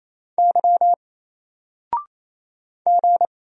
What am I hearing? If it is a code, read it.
Morse code, heard as a keyed tone, about 27 wpm, 700 Hz. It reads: Y G
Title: Single Letter-Number 28wpm